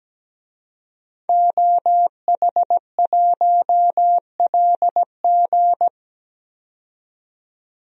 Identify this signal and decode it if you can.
Morse code, heard as a keyed tone, about 17 wpm, 700 Hz. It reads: OH1LG
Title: DXCC-100 Call Signs - 5 Characters - 17wpm